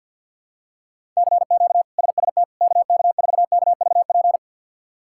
Morse code, heard as a keyed tone, about 48 wpm, 700 Hz. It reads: CQ SST KK4KVP
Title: K1USN SST QSO Practice - 50wpm